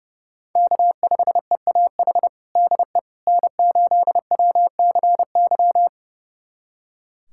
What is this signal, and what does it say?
Morse code, heard as a keyed tone, about 32 wpm, 700 Hz. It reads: K5EAH DE N8WCY
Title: USA Callsign DE Callsign - 5 Characters - 30wpm